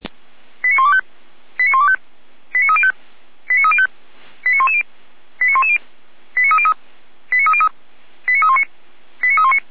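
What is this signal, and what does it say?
Five-tone selective call (ZVEI), heard as quick runs of five tones, tones of about 80 ms. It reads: 89125 89125 89305 89305 89110 89110 89303 89303 89319 89319
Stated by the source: Five-tone ZVEI selective call received with an SDR.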